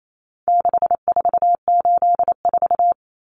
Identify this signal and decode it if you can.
Morse code, heard as a keyed tone, about 28 wpm, 700 Hz. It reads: 6484
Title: Numbers - 4-Digit 28wpm